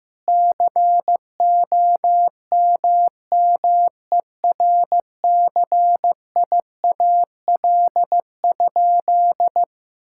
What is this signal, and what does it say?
Morse code, heard as a keyed tone, about 15 wpm, 700 Hz. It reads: COMMERCIAL?
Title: Sentences from Top 1000 Words - Review Part 5 - 15wpm